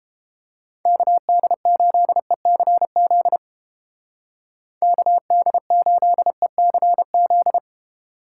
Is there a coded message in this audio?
Yes: Morse code from 0.9 to 7.6 s, heard as a keyed tone, about 31 wpm, 700 Hz. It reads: KD8ECZ KD8ECZ